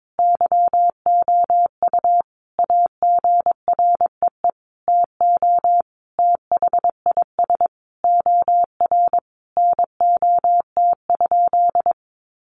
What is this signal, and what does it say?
Morse code, heard as a keyed tone, about 22 wpm, 700 Hz. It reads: YOU AGREE TO THIS OR NOT?
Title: Sentences from Top 1100 Words 22wpm